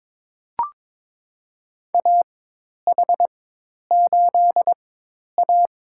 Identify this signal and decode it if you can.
Morse code, heard as a keyed tone, about 22 wpm, spaced as for 10 wpm, 700 Hz. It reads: AH8A
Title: USA Call Signs - 4 Characters - 4x Character Spacing - 22wpm